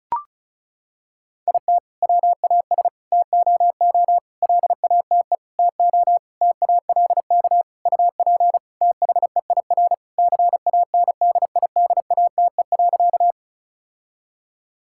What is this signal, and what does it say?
Morse code, heard as a keyed tone, about 36 wpm, 700 Hz. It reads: IT WAS TOO LATE TO TALK UP THEIR CANDIDATE.
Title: Sentences from Top 1600 Words 35wpm